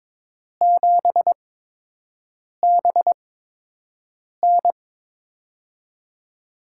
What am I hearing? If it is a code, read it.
Morse code, heard as a keyed tone, about 21 wpm, spaced as for 6 wpm, 700 Hz. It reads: 7BN
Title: USA Call Signs - 4 Characters - 8x Character Spacing - 22wpm